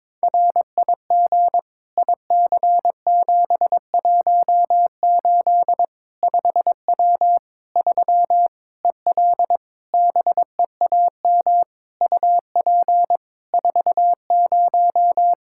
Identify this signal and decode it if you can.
Morse code, heard as a keyed tone, about 22 wpm, 700 Hz. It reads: RIG IC718 5W 3 EL BEAM UP 40
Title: K1USN SST QSO Practice 22wpm